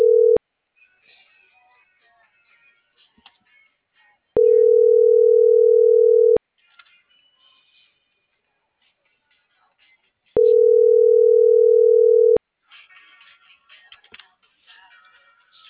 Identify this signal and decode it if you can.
Ringback tone (North America), 440 + 480 Hz.